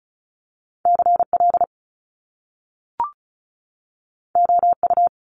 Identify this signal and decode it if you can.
Morse code, heard as a keyed tone, about 36 wpm, 700 Hz. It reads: CL OU
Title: Top Two Letter Combinations in Words 35wpm